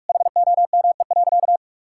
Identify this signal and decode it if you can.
Morse code, heard as a keyed tone, about 41 wpm, 700 Hz. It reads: HOME.
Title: Sentences from Top 1000 Words - Review Part 5 - 45wpm